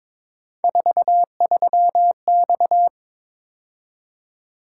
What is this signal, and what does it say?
Morse code, heard as a keyed tone, about 22 wpm, 700 Hz. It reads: 43X
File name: practice / 22wpm / Three Letter-Number Combinations - 22wpm